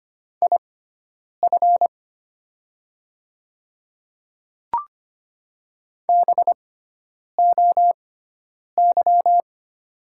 Morse code, heard as a keyed tone, about 25 wpm, spaced as for 9 wpm, 700 Hz. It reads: IF BOY
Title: 6x Character Spacing - Top 500 Words - 25wpm